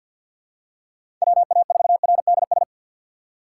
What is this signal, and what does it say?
Morse code, heard as a keyed tone, about 51 wpm, 700 Hz. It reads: WA4RDS
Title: USA Call Signs - 3x Character Spacing - 50wpm